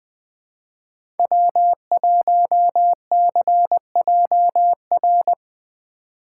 Morse code, heard as a keyed tone, about 21 wpm, 700 Hz. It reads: W1CJR
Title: USA Call Signs - 3x Character Spacing - 20wpm